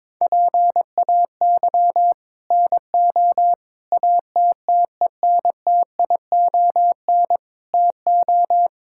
Morse code, heard as a keyed tone, about 23 wpm, 700 Hz. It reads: PAY NO ATTENTION TO
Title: Sentences from Top 1000 Words - Review Part 6 - 22wpm